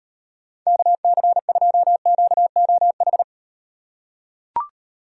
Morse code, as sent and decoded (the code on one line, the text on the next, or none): -.- -.-. ..--- --.- --- ....
KC2QOH